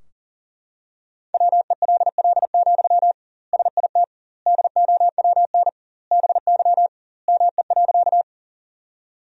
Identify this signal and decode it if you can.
Morse code, heard as a keyed tone, about 40 wpm, 700 Hz. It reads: WELL, SIT DOWN BY ME.
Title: 2x Word-Spacing - Sentences from Top 500 Words - 40wpm